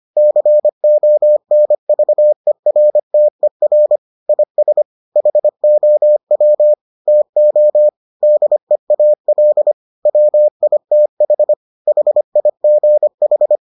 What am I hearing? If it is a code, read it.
Morse code, heard as a keyed tone, about 25 wpm, 600 Hz. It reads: CONVERTER IS HOW TO DEAL WITH HIGH